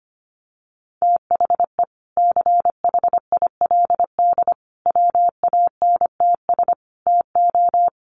Morse code, heard as a keyed tone, about 25 wpm, 700 Hz. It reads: THE CHILD WANTS TO
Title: Additional Easy Phrases 25wpm